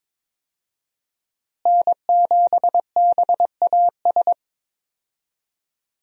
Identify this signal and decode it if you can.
Morse code, heard as a keyed tone, about 22 wpm, 700 Hz. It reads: N7BAS
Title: USA Call Signs - 5 Characters - Repeated 2x - 22wpm